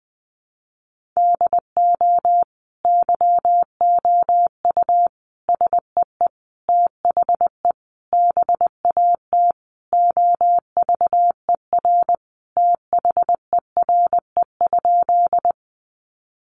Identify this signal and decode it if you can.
Morse code, heard as a keyed tone, about 20 wpm, 700 Hz. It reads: DO YOU SEE THE BAT OVER THERE?
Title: Sentences from Top 1500 Words 20wpm